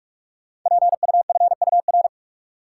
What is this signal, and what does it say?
Morse code, heard as a keyed tone, about 44 wpm, 690 Hz. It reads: PUFUR
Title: Code Groups - 5 Letters - 45wpm